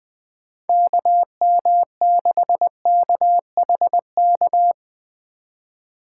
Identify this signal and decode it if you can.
Morse code, heard as a keyed tone, about 23 wpm, 700 Hz. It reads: KM6KHK
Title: Speed Racing - USA 2x3 Call Signs 20wpm